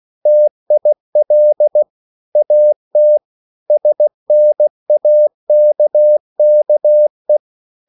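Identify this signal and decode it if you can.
Morse code, heard as a keyed tone, about 16 wpm, 600 Hz. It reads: TIL AT SNAKKE
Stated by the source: Slow (QRS) Morse code at 16 wpm, from CW med Gnister.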